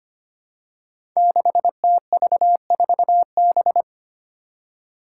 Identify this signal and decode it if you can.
Morse code, heard as a keyed tone, about 25 wpm, 700 Hz. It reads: 6TV4B
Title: Five Challenging Letter-Number Combinations 25wpm